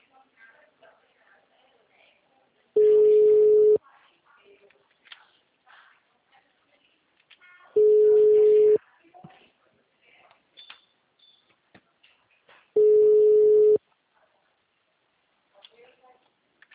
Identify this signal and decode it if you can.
Ringback tone (Europe, 425 Hz), with speech.